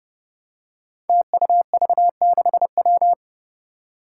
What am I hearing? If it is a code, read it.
Morse code, heard as a keyed tone, about 31 wpm, 700 Hz. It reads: TUV6W
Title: Five Challenging Letter-Number Combinations 30wpm